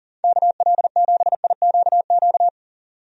Morse code, heard as a keyed tone, about 40 wpm, 700 Hz. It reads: KL7IQQ
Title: DXCC-100 Call Signs - 6 Characters - 40wpm